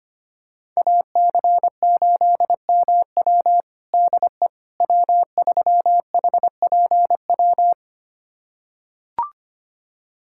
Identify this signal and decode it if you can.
Morse code, heard as a keyed tone, about 25 wpm, 700 Hz. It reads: AC8MW DE W3HPW
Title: USA Callsign DE Callsign - 5 Characters - 25wpm